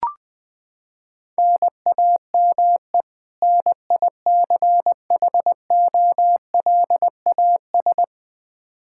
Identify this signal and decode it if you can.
Morse code, heard as a keyed tone, about 20 wpm, 700 Hz. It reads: NAME NICHOLAS
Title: K1USN SST QSO Practice 20wpm